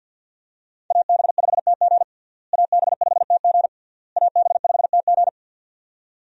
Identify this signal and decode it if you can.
Morse code, heard as a keyed tone, about 49 wpm, 700 Hz. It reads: AB5TG AB5TG AB5TG
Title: USA Call Signs - 5 Characters - Repeated 3x - 50wpm